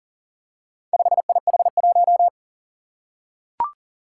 Morse code, heard as a keyed tone, about 40 wpm, 700 Hz. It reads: 5IH1
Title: Four Challenging Letter-Number Combinations 40wpm